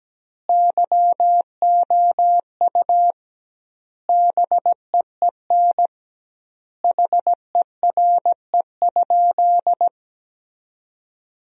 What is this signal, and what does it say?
Morse code, heard as a keyed tone, about 17 wpm, 700 Hz. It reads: YOU BEEN HERE?
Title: 2x Word-Spacing - Sentences from Top 200 Words - 17wpm